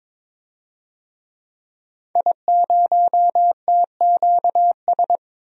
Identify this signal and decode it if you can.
Morse code, heard as a keyed tone, about 22 wpm, 700 Hz. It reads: I0TQS
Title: DXCC-100 Call Signs - 5 Characters - 22wpm